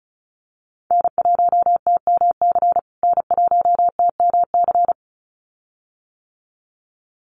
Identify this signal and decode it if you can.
Morse code, heard as a keyed tone, about 34 wpm, 700 Hz. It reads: N1TMC N1TMC
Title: USA Call Signs Repeated 2x 35wpm